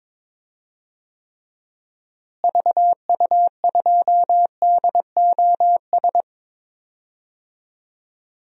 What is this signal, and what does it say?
Morse code, heard as a keyed tone, about 22 wpm, 700 Hz. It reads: VU2DOS
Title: DXCC-100 Call Signs - 6 Characters - Repeated 2x - 22wpm